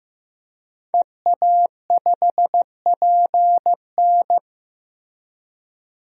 Morse code, heard as a keyed tone, about 16 wpm, 700 Hz. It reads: EA5PN
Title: DXCC-100 Call Signs - 5 Characters - Repeated 2x - 15wpm